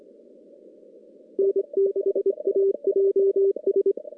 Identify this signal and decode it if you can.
Morse code, heard as a keyed tone, about 25 wpm, 410 Hz. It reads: N6AJS
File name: contests / CW-SS-k5tr- -2300